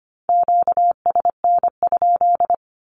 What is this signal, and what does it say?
Morse code, heard as a keyed tone, about 25 wpm, 700 Hz. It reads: QSN?
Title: Q-Codes Repeated 3x 25wpm